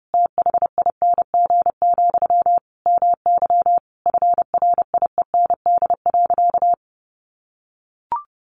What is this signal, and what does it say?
Morse code, heard as a keyed tone, about 30 wpm, 700 Hz. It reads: THING, MY FRIEND.
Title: Sentences from Top 1200 Words 30wpm